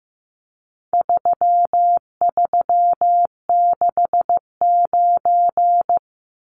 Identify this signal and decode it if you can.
Morse code, heard as a keyed tone, about 15 wpm, 700 Hz. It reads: 3369
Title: Numbers - 4-Digit 15wpm